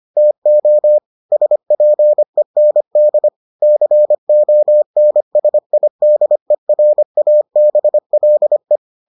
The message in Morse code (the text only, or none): TO SPEND CONSIDERABLE